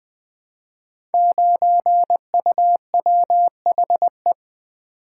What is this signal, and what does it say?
Morse code, heard as a keyed tone, about 20 wpm, 700 Hz. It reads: 9UWHE
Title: Code Groups - 5 Letter:Numbers - 20wpm